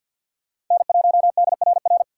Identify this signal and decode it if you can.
Morse code, heard as a keyed tone, about 46 wpm, 700 Hz. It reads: N1DRR
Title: USA Call Signs - 5 Characters - 2x Character Spacing - 50wpm